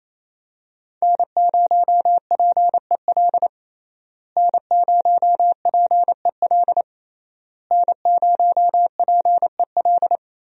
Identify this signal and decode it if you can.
Morse code, heard as a keyed tone, about 27 wpm, 700 Hz. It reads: N0PEL N0PEL N0PEL
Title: USA Call Signs - 5 Characters - Repeated 3x - 28wpm